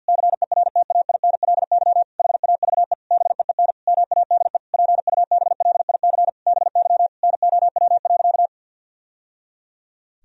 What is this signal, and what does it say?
Morse code, heard as a keyed tone, about 52 wpm, 700 Hz. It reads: CERTAINLY HAVE BEEN MADE PUBLIC BY NOW.
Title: Sentences from Top 1000 Words - Review Part 3 - 50wpm